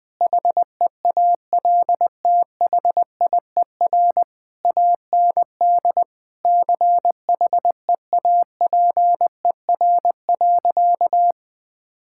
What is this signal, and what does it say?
Morse code, heard as a keyed tone, about 19 wpm, 700 Hz. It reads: HEALTHIER AND CHEAPER.
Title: Difficult Sentences with Hints - 20wpm